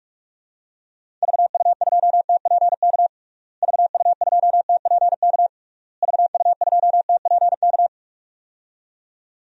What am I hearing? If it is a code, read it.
Morse code, heard as a keyed tone, about 46 wpm, 700 Hz. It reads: VU2TPK VU2TPK VU2TPK
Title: DXCC-100 Call Signs - 6 Characters - Repeated 3x - 45wpm